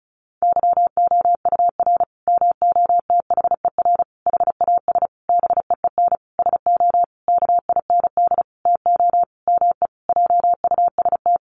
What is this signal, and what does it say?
Morse code, heard as a keyed tone, about 34 wpm, 700 Hz. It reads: YOUR MOTHER HAS BEEN SO KIND TO ME JUST
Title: Sentences from Top 1200 Words 35wpm